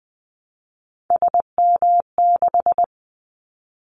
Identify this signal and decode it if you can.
Morse code, heard as a keyed tone, about 20 wpm, 700 Hz. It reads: SM6
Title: Callsign Prefixes for Top 25 DXCC - Repeated 3x - 20wpm